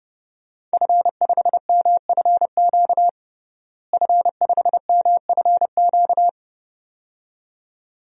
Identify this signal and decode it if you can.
Morse code, heard as a keyed tone, about 30 wpm, 700 Hz. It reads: F5MFQ F5MFQ
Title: DXCC-100 Call Signs - 5 Characters - Repeated 2x - 30wpm